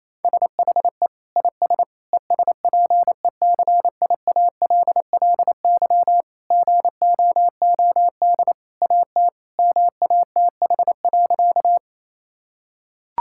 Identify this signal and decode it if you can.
Morse code, heard as a keyed tone, about 28 wpm, 700 Hz. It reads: SHE IS ESPECIALLY GOOD AT MATH.
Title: Sentences from Top 1000 Words - Review Part 6 - 28wpm